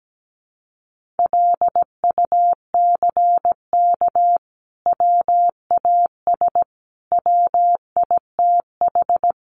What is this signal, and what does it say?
Morse code, heard as a keyed tone, about 17 wpm, 700 Hz. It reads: LUCK WAS WITH
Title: Sentences from Top 1500 Words 17wpm